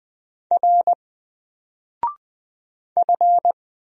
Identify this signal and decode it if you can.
Morse code, heard as a keyed tone, about 20 wpm, 700 Hz. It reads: R F
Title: Single Character (Letter:Number:Punctuation:Prosign) 20wpm